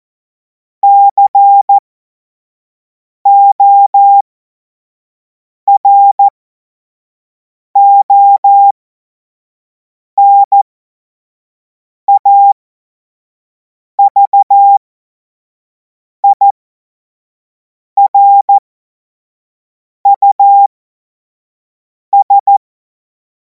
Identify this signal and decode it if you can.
Morse code, heard as a keyed tone, about 14 wpm, spaced as for 5 wpm, 800 Hz. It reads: CORONAVIRUS
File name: Morsekurs / Witze / Witz 25. August 2026 25 F70